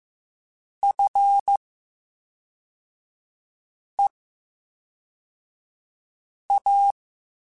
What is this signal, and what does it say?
Morse code, heard as a keyed tone, about 15 wpm, 800 Hz. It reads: F E A
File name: Special INGs / Special INGs-mp3s / VerySlowToFaster Ing @10